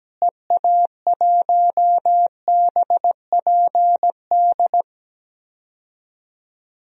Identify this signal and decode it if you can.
Morse code, heard as a keyed tone, about 17 wpm, 700 Hz. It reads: EA1BPD
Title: DXCC-100 Call Signs - 6 Characters - Repeated 2x - 17wpm